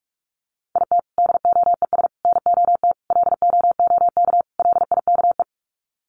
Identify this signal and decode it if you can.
Morse code, heard as a keyed tone, about 45 wpm, 700 Hz. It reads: IT DOES NOT LOOK LIKE
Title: Sets of 5 Words Encore2 45wpm